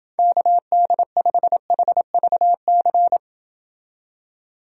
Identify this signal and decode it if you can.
Morse code, heard as a keyed tone, about 29 wpm, 700 Hz. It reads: KD5HVC